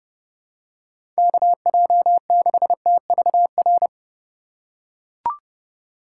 Morse code, heard as a keyed tone, about 30 wpm, 700 Hz. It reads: KJ6TVR
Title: USA Call Signs - 6 Characters - 4x Character Spacing - 30wpm